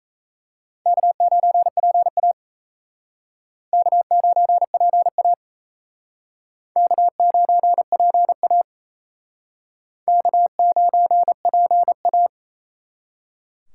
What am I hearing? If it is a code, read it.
Morse code, heard as a keyed tone, about 36 wpm, 700 Hz. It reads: K9PA K9PA K9PA K9PA